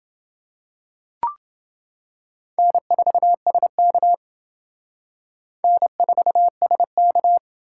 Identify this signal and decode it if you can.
Morse code, heard as a keyed tone, about 29 wpm, 700 Hz. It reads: N4SK N4SK